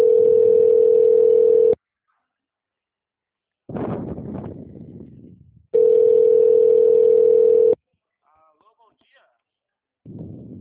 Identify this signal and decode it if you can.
Ringback tone (North America), 440 + 480 Hz, with speech.
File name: out- -2022- - - .49276